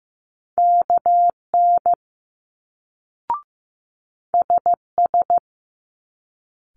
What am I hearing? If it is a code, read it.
Morse code, heard as a keyed tone, about 15 wpm, 700 Hz. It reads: KN SS
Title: Multi-letter Phonemes Quick Fire 15wpm